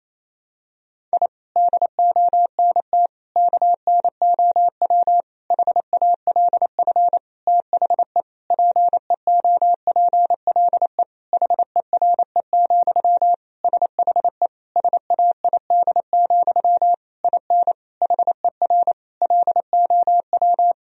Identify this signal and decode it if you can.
Morse code, heard as a keyed tone, about 27 wpm, 700 Hz. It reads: I DONT KNOW HALF THE PEOPLE HERE, SHE SAID, IN HER LOW
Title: Sentences from Top 1000 Words - Review Part 1 - 28wpm